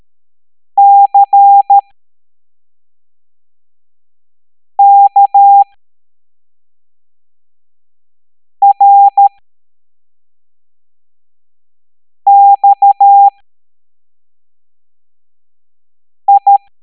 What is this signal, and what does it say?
Morse code, heard as a keyed tone, about 13 wpm, spaced as for 3 wpm, 800 Hz. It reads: CKRXI